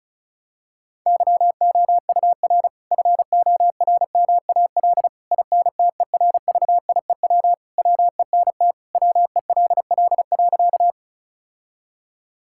Morse code, heard as a keyed tone, about 36 wpm, 700 Hz. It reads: YOUR FORMAL INTERVIEW WENT WELL.
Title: Sentences from Top 1600 Words 35wpm